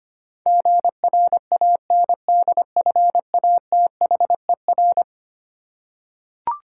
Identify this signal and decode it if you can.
Morse code, heard as a keyed tone, about 25 wpm, 700 Hz. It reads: GRANDFATHER
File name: practice / 25wpm / Eleven Letter Words - 25wpm